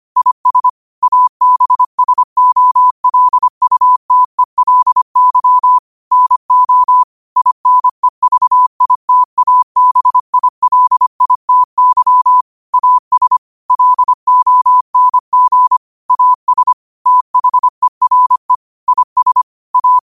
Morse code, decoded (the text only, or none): IS ABSOLUTELY NO INEVITABILITY AS LONG AS THERE IS A